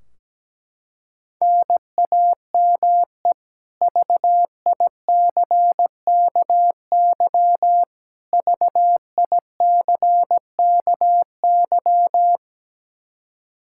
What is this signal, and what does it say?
Morse code, heard as a keyed tone, about 17 wpm, 700 Hz. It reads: NAME VICKY VICKY
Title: K1USN SST QSO Practice 17wpm